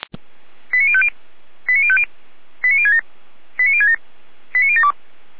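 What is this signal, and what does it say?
Five-tone selective call (ZVEI), heard as quick runs of five tones, tones of about 70 ms. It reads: 89050 89050 89067 89067 89072
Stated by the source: ZVEI five-tone selective call received on SDR.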